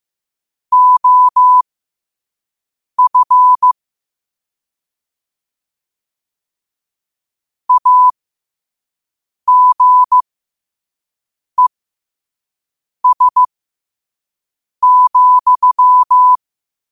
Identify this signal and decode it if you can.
Morse code, heard as a keyed tone, about 15 wpm, spaced as for 5 wpm, 1000 Hz. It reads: OF AGES,
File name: qotdfiles / 5wpm